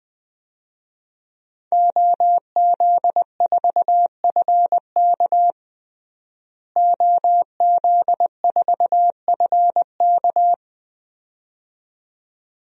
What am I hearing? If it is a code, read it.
Morse code, heard as a keyed tone, about 20 wpm, 700 Hz. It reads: OZ4FK OZ4FK
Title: DXCC-100 Call Signs - 5 Characters - Repeated 2x - 20wpm